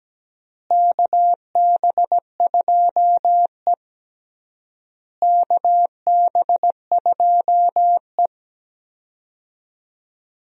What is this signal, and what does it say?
Morse code, heard as a keyed tone, about 17 wpm, 700 Hz. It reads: KB2E KB2E
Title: USA Call Signs - 4 Characters - Repeated 3x - 17wpm